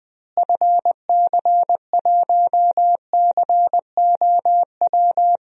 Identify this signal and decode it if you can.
Morse code, heard as a keyed tone, about 20 wpm, 700 Hz. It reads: FC1COW
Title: DXCC-100 Call Signs - 6 Characters - Repeated 3x - 20wpm